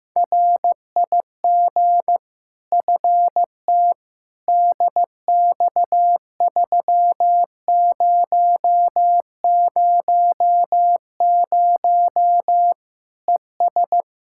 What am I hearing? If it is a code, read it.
Morse code, heard as a keyed tone, about 15 wpm, 700 Hz. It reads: RIG FT DX3000 ES
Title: Practice QSO - Exchange 2 - Info - 15wpm